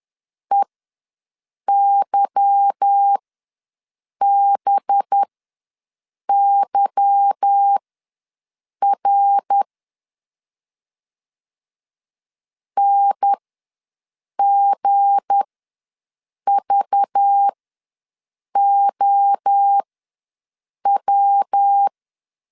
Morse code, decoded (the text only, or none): EYBYR NGVOW